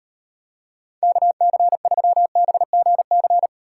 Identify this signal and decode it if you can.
Morse code, heard as a keyed tone, about 38 wpm, 700 Hz. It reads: KC3BGC